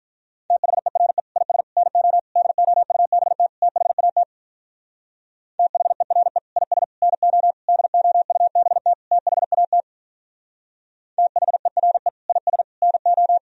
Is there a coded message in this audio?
Yes: Morse code from 0.5 to 13.4 s, heard as a keyed tone, about 46 wpm, 700 Hz. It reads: THERE IS NO DOUBT THAT THERE IS NO DOUBT THAT THERE IS NO